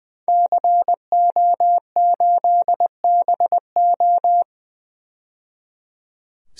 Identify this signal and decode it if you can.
Morse code, heard as a keyed tone, about 20 wpm, 700 Hz. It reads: CO8BO
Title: DXCC-100 Call Signs - 5 Characters - 20wpm